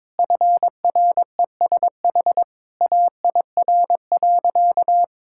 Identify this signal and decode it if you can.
Morse code, heard as a keyed tone, about 22 wpm, 700 Hz. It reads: FRESH AIR.
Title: Sentences from Top 1600 Words 22wpm